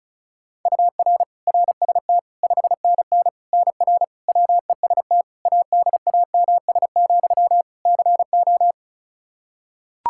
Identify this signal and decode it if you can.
Morse code, heard as a keyed tone, about 34 wpm, 700 Hz. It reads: UR RST 5NN NR WEST ADAMS, CO
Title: Practice QSO - Exchange 1 - Beginning - 35wpm